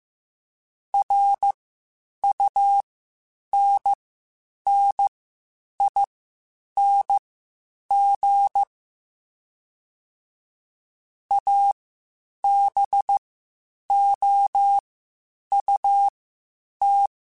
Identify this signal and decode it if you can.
Morse code, heard as a keyed tone, about 15 wpm, spaced as for 8 wpm, 800 Hz. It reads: RUNNING ABOUT
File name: QSOs / QSOs-mp3s / WD0FBT 1@13